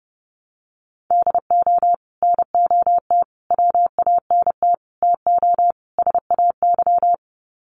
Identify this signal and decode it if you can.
Morse code, heard as a keyed tone, about 30 wpm, 700 Hz. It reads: DO NOT WANT TO SAY
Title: Sets of 5 Words Encore 30wpm